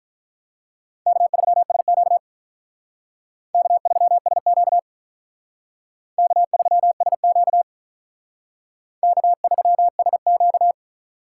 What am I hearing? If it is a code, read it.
Morse code, heard as a keyed tone, about 45 wpm, 700 Hz. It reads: K3SQ K3SQ K3SQ K3SQ